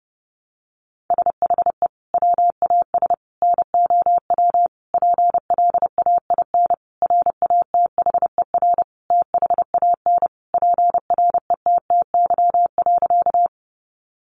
Morse code, heard as a keyed tone, about 30 wpm, 700 Hz. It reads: SHE WAS NOW PLAIN RATHER THAN PRETTY.
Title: Sentences from Top 1000 Words 30wpm